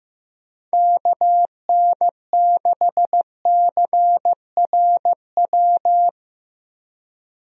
Morse code, heard as a keyed tone, about 15 wpm, 700 Hz. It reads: KN6CRW
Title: USA Call Signs - 6 Characters - 15wpm